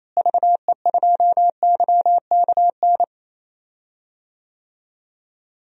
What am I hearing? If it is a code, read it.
Morse code, heard as a keyed tone, about 28 wpm, 700 Hz. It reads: VE2YKN
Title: DXCC-100 Call Signs - 6 Characters - Repeated 3x - 28wpm